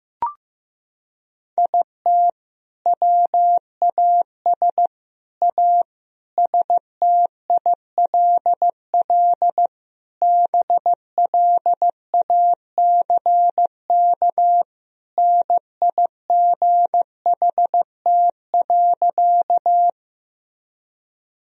Morse code, heard as a keyed tone, about 15 wpm, 700 Hz. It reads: IT WAS A STILL BLACK NIGHT.
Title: Sentences from Top 1000 Words - Review Part 2 - 15wpm